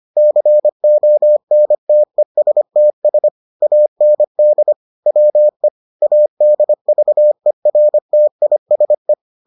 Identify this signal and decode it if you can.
Morse code, heard as a keyed tone, about 25 wpm, 600 Hz. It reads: CONTESTS AND WE ADVERTISE